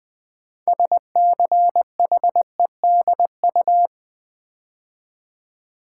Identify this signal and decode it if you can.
Morse code, heard as a keyed tone, about 20 wpm, 700 Hz. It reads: SCHEDU
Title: Build A Word - 3 Letters At A Time - 8-Letter Words - 20wpm